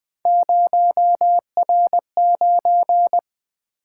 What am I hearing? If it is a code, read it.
Morse code, heard as a keyed tone, about 20 wpm, 700 Hz. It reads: 0R9
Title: Three Letter-Number Combinations - 20wpm